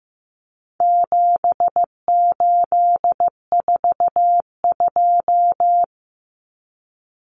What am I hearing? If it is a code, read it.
Morse code, heard as a keyed tone, about 15 wpm, 700 Hz. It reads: 7842
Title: Numbers - 4-Digit 15wpm